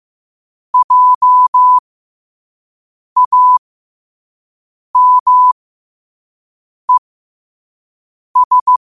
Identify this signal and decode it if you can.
Morse code, heard as a keyed tone, about 15 wpm, spaced as for 5 wpm, 1000 Hz. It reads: JAMES
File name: qotdfiles / 5wpm